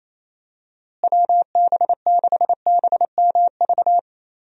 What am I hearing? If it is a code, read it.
Morse code, heard as a keyed tone, about 28 wpm, 700 Hz. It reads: WB6BMV
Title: USA Call Signs - 6 Characters - Repeated 2x - 28wpm